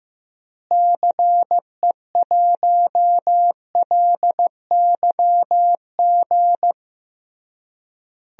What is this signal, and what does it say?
Morse code, heard as a keyed tone, about 16 wpm, 700 Hz. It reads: CE1LYG
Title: DXCC-100 Call Signs - 6 Characters - Repeated 3x - 15wpm